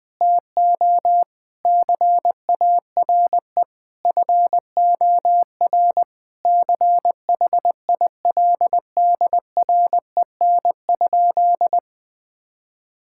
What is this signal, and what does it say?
Morse code, heard as a keyed tone, about 20 wpm, 700 Hz. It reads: TO CARE FOR CHILDREN?
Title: Sentences from Top 1000 Words - Review Part 5 - 20wpm